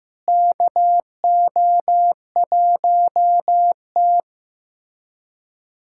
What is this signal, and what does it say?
Morse code, heard as a keyed tone, about 15 wpm, 700 Hz. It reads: KO1T
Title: USA Callsign DE Callsign - 4 Characters - 15wpm